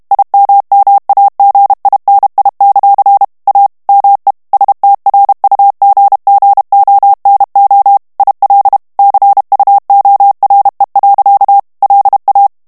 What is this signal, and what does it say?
Morse code, heard as a keyed tone, about 32 wpm, 800 Hz. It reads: IMMAGINI; A ME STRUGGONO IL CUORE. LA
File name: ITALIEN texte clair 1 vitesse 30 mots minute